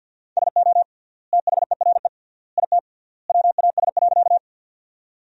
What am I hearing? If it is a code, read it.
Morse code, heard as a keyed tone, about 51 wpm, 700 Hz. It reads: SO THERE IT WAS.
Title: 3x Word-Spacing - Sentences from Top 100 Words - 50wpm